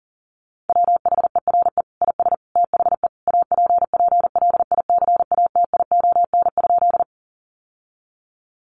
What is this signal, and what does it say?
Morse code, heard as a keyed tone, about 40 wpm, 700 Hz. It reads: WHERE IS THE APPLICATION?
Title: Sentences from Top 1200 Words 40wpm